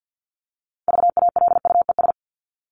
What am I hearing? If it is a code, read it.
Morse code, heard as a keyed tone, about 50 wpm, 700 Hz. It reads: VALUES
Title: ES:S Suffix - Top 100 Words with Base Words 50wpm